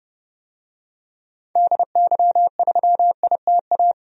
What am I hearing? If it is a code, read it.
Morse code, heard as a keyed tone, about 30 wpm, 700 Hz. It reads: DY3ITA
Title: DXCC-100 Call Signs - 6 Characters - Repeated 3x - 30wpm